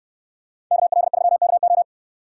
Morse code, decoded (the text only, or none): KL3KO